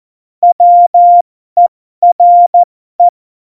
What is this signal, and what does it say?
Morse code, heard as a keyed tone, about 14 wpm, spaced as for 12 wpm, 700 Hz. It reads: WERE
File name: Morsekurs / Witze